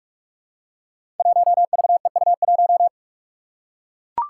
Morse code, heard as a keyed tone, about 44 wpm, 700 Hz. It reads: 1VEU1
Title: Five Challenging Letter-Number Combinations 45wpm